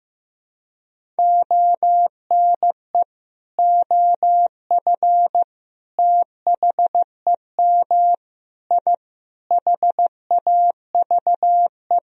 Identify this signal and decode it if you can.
Morse code, heard as a keyed tone, about 15 wpm, 700 Hz. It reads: ONE OF THEM I HAVE
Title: Sentences from Top 1000 Words - Review Part 1 - 15wpm